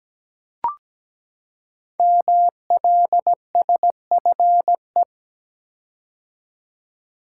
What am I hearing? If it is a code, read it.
Morse code, heard as a keyed tone, about 17 wpm, 700 Hz. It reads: MLSFE
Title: Code Groups - 5 Letters - 17wpm